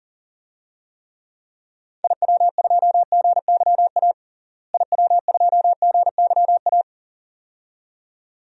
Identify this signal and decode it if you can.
Morse code, heard as a keyed tone, about 37 wpm, 700 Hz. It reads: IW2GYA IW2GYA
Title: DXCC-100 Call Signs - 6 Characters - Repeated 2x - 40wpm